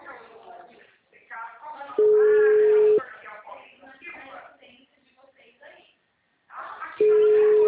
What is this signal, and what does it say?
Ringback tone (Europe, 425 Hz), with speech.
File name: out- -2007- - - .53022